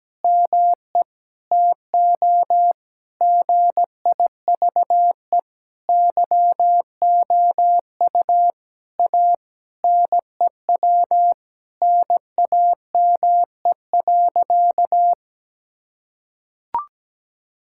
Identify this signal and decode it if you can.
Morse code, heard as a keyed tone, about 17 wpm, 700 Hz. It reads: ME TO GIVE YOU A NEW NAME.
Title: Sentences from Top 1000 Words - Review Part 5 - 17wpm